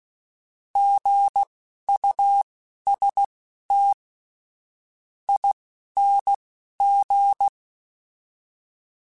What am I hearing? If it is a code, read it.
Morse code, heard as a keyed tone, about 16 wpm, spaced as for 10 wpm, 800 Hz. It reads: GUSTING
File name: Alphabetical / Alphabetical mp3s / Gs @10